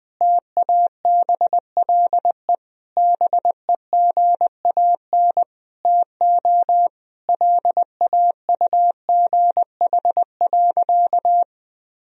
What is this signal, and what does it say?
Morse code, heard as a keyed tone, about 20 wpm, 700 Hz. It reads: TABLE BEGAN TO LAUGH.
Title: Sentences from Top 1000 Words - Review Part 2 - 20wpm